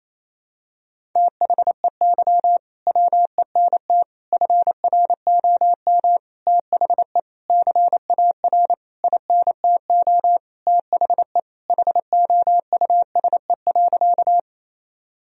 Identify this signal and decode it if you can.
Morse code, heard as a keyed tone, about 28 wpm, 700 Hz. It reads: THEY WENT FROM THE CAR INTO THE HOUSE.
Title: Sentences from Top 1000 Words - Review Part 1 - 28wpm